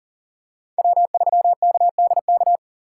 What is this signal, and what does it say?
Morse code, heard as a keyed tone, about 44 wpm, 690 Hz. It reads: W3KDK
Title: USA Call Signs - 5 Characters - 6x Character Spacing - 40wpm